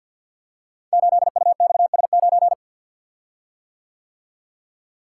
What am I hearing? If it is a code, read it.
Morse code, heard as a keyed tone, about 48 wpm, 700 Hz. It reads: 8UXS9
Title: Code Groups - 5 Letter:Numbers - 50wpm